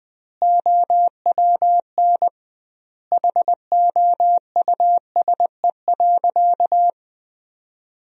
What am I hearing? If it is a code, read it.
Morse code, heard as a keyed tone, about 21 wpm, 700 Hz. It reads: OWN HOUSE.
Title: 2x Word-Spacing - Sentences from Top 300 Words - 20wpm